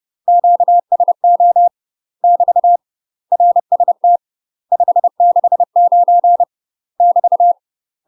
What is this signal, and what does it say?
Morse code, heard as a keyed tone, about 30 wpm, 700 Hz. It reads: QSO <BT> RST 569 <BT>
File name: UK MASTER QSOs TRAINING / 30WPM / UK MASTER QSOs 0007